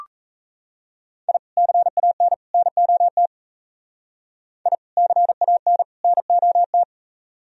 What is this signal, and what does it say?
Morse code, heard as a keyed tone, about 33 wpm, 700 Hz. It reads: I CAN NOT I CAN NOT
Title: Speed-Racing - Sets of 3 Words 28wpm